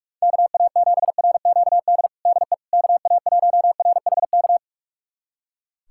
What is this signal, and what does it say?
Morse code, heard as a keyed tone, about 42 wpm, 700 Hz. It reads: KA7RQD DE KA1RHK
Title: USA Callsign DE Callsign - 6 Characters - 45wpm